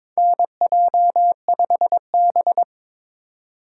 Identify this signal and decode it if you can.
Morse code, heard as a keyed tone, about 22 wpm, 700 Hz. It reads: NJ5B
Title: USA Call Signs - 2x Character Spacing - 22wpm